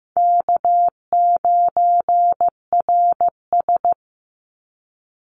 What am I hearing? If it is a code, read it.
Morse code, heard as a keyed tone, about 15 wpm, 700 Hz. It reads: K9RS
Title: Easy Call Signs - 15wpm